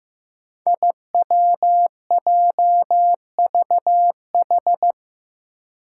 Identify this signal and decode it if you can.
Morse code, heard as a keyed tone, about 15 wpm, 700 Hz. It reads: IWJVH
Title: Five Challenging Letter-Number Combinations 15wpm